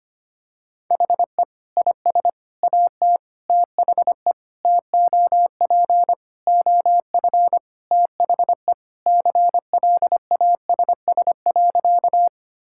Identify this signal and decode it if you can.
Morse code, heard as a keyed tone, about 25 wpm, 700 Hz. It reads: HE IS AT THE TOP OF THE CLASS.
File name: practice / 25wpm / Sentences with 3-Word Phrases - 25wpm